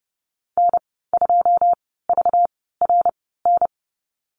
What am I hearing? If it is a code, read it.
Morse code, heard as a keyed tone, about 30 wpm, spaced as for 17 wpm, 700 Hz. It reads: N2VRN
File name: Lesson 37g - Letter x - Call Signs 30wpm